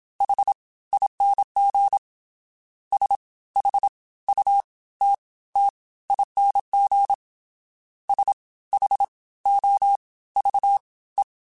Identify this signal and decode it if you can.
Morse code, heard as a keyed tone, about 27 wpm, spaced as for 17 wpm, 800 Hz. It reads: HING SHUTTING SHOVE